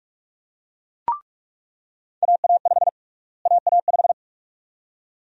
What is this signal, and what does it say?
Morse code, heard as a keyed tone, about 41 wpm, 700 Hz. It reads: AA5 AA5
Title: USA Call Sign Prefixes - Repeated 2x - 45wpm